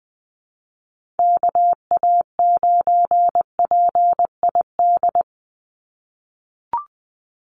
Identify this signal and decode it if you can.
Morse code, heard as a keyed tone, about 20 wpm, 700 Hz. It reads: KA9PID
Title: USA Call Signs 20wpm